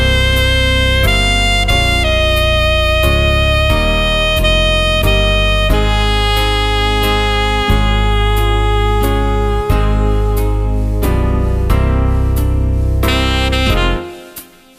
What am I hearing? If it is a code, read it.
Dial-up modem handshake.